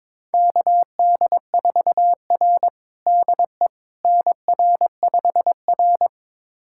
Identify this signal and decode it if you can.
Morse code, heard as a keyed tone, about 22 wpm, 700 Hz. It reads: KD4R DE NR5R
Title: USA Callsign DE Callsign - 4 Characters - 22wpm